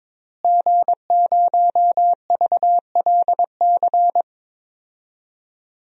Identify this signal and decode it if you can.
Morse code, heard as a keyed tone, about 22 wpm, 700 Hz. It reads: G0VLC
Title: DXCC-100 Call Signs - 5 Characters - Repeated 3x - 22wpm